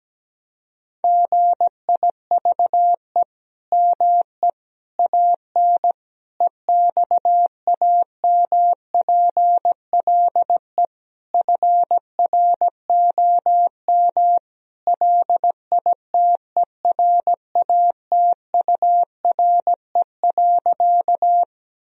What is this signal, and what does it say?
Morse code, heard as a keyed tone, about 17 wpm, 700 Hz. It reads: GIVE ME AN EXAMPLE FROM LITERATURE.
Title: Sentences from Top 1000 Words - Review Part 3 - 17wpm